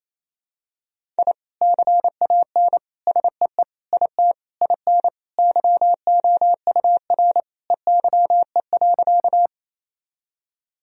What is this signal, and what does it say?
Morse code, heard as a keyed tone, about 27 wpm, 700 Hz. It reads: I CAN SEE IT IN YOUR EYE.
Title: Sentences from Top 1000 Words - Review Part 1 - 28wpm